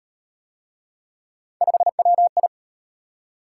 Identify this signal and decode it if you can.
Morse code, heard as a keyed tone, about 38 wpm, 700 Hz. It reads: 5WI